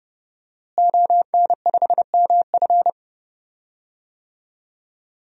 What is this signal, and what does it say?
Morse code, heard as a keyed tone, about 30 wpm, 700 Hz. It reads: ON5MF
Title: DXCC-100 Call Signs - 5 Characters - Repeated 3x - 30wpm